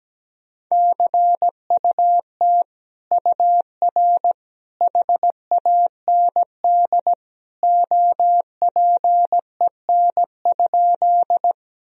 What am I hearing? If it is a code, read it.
Morse code, heard as a keyed tone, about 17 wpm, 700 Hz. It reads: CUT UR HAND OPEN?
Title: Sentences from Top 300 Words and Abbreviations 17wpm